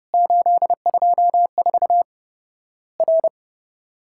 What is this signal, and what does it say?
Morse code, heard as a keyed tone, about 30 wpm, 700 Hz. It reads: 824 R
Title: Worked All Europe DX Contest - QTC Exchange - 30wpm